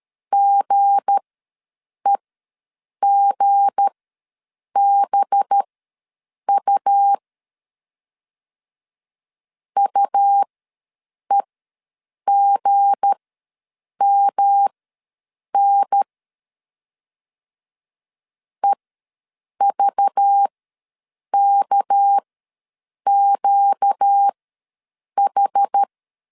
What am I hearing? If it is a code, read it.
Morse code, heard as a keyed tone, about 13 wpm, spaced as for 7 wpm, 800 Hz. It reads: GEGBU UEGMN EVKQH
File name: LEZIONI / 7 GF9Z / 7 G ws 12wpm 3140